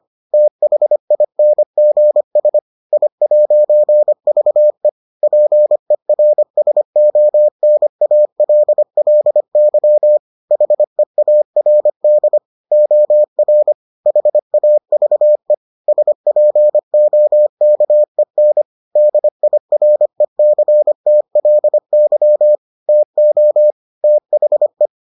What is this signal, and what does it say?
Morse code, heard as a keyed tone, about 25 wpm, 600 Hz. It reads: THINGS I'VE PERSONALLY HEARD OR HAVE SPOKEN DIRECTLY TO THE